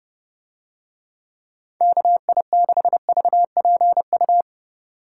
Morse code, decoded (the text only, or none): KI6VPU